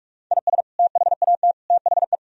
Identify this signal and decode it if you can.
Morse code, heard as a keyed tone, about 46 wpm, 700 Hz. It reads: IS THAT THE
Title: Sentences from Top 1000 Words - Review Part 3 - 45wpm